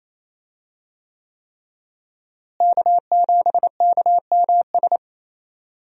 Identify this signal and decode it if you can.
Morse code, heard as a keyed tone, about 28 wpm, 700 Hz. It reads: K7KMS
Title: USA Call Signs - 5 Characters - Repeated 3x - 28wpm